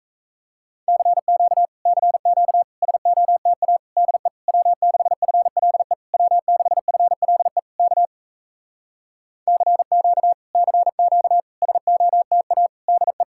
Morse code, heard as a keyed tone, about 37 wpm, 700 Hz. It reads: CQ CQ SOTA DE W6FLE W6FLE K CQ CQ SOTA DE